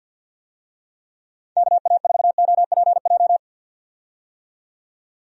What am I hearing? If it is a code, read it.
Morse code, heard as a keyed tone, about 51 wpm, 700 Hz. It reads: KA4OPJ
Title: USA Call Signs - 6 Characters - 50wpm